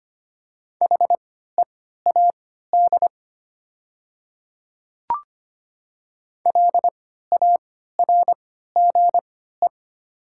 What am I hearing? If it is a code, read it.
Morse code, heard as a keyed tone, about 25 wpm, spaced as for 14 wpm, 700 Hz. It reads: HEAD LARGE
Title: 3x Character Spacing - Top 300 Words - 25wpm